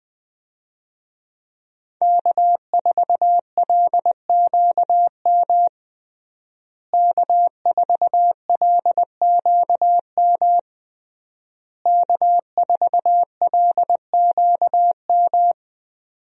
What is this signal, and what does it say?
Morse code, heard as a keyed tone, about 19 wpm, 700 Hz. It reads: K4LQM K4LQM K4LQM
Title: USA Call Signs - 5 Characters - Repeated 3x - 20wpm